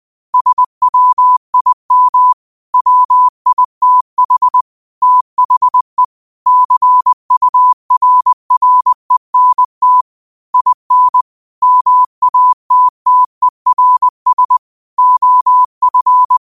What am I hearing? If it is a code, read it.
Morse code, heard as a keyed tone, about 20 wpm, 1000 Hz. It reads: SWIM WITH THE CURRENT IN MATTERS OF